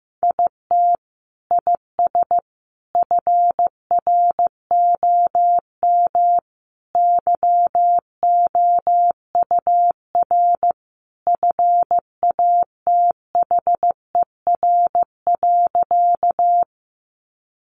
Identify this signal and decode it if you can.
Morse code, heard as a keyed tone, about 15 wpm, 700 Hz. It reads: IT IS FROM YOUR FATHER.
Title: Sentences from Top 300 Words 15wpm